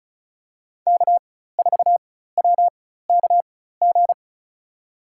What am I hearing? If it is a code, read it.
Morse code, heard as a keyed tone, about 34 wpm, spaced as for 16 wpm, 700 Hz. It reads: K4WKG
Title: USA Call Signs - 5 Characters - 4x Character Spacing - 35wpm